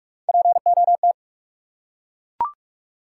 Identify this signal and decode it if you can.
Morse code, heard as a keyed tone, about 43 wpm, 700 Hz. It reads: POT
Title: Three Letter Words - 45wpm